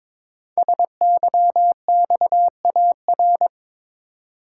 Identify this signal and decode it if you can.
Morse code, heard as a keyed tone, about 22 wpm, 700 Hz. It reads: SYXAR
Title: Code Groups - 5 Letters - 22wpm